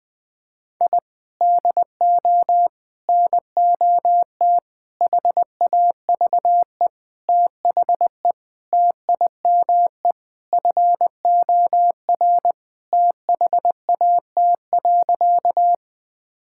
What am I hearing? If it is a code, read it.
Morse code, heard as a keyed tone, about 21 wpm, 700 Hz. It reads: I DO NOT HAVE THE TIME FOR THAT.
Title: Sentences with 5-Word Phrases - 20wpm